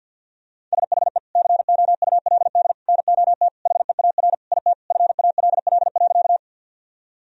Morse code, heard as a keyed tone, about 50 wpm, 700 Hz. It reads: SHE COULD NOT HEAR IT FALL.
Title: Sentences from Top 1000 Words - Review Part 2 - 50wpm